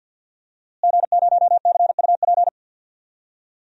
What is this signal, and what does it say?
Morse code, heard as a keyed tone, about 48 wpm, 700 Hz. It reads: G0CUP
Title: DXCC-100 Call Signs - 5 Characters - Repeated 3x - 50wpm